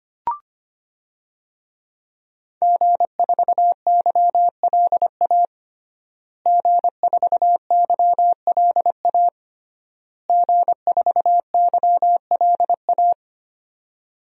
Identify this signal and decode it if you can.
Morse code, heard as a keyed tone, about 25 wpm, 700 Hz. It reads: G4YLA G4YLA G4YLA
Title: DXCC-100 Call Signs - 5 Characters - Repeated 3x - 25wpm